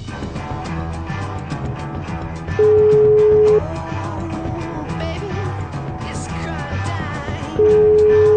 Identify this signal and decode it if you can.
Ringback tone (Europe, 425 Hz).